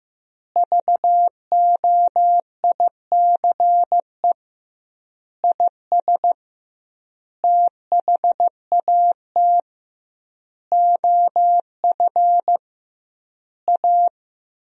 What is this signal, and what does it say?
Morse code, heard as a keyed tone, about 15 wpm, 700 Hz. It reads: VOICE IS THAT OF A
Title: 2x Word-Spacing - Sentences from Top 500 Words - 15wpm